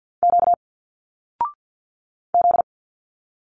Morse code, heard as a keyed tone, about 46 wpm, 700 Hz. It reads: Q Z
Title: Single Letters 50wpm